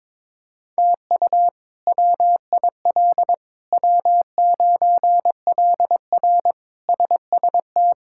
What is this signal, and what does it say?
Morse code, heard as a keyed tone, about 22 wpm, 700 Hz. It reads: TU WIL W9LR SST
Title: K1USN SST QSO Practice - 22wpm